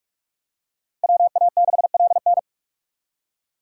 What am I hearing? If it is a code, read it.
Morse code, heard as a keyed tone, about 40 wpm, 700 Hz. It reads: WA6LN